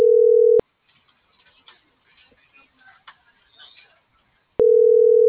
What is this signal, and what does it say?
Ringback tone (North America), 440 + 480 Hz.